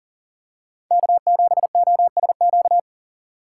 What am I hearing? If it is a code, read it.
Morse code, heard as a keyed tone, about 40 wpm, 700 Hz. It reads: K7OSQ
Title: USA Call Signs - 5 Characters - 6x Character Spacing - 40wpm